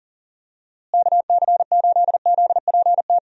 Morse code, heard as a keyed tone, about 40 wpm, 700 Hz. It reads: KC8ZPT
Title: USA Call Signs - 6 Characters - Repeated 3x - 40wpm